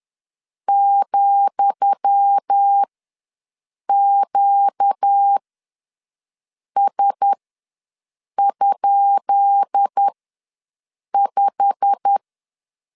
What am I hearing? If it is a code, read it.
Morse code, heard as a keyed tone, about 10 wpm, spaced as for 6 wpm, 800 Hz. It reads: ,QS?5
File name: LEZIONI / 10 SIGNS / 10D COMA ws 10wpm 4314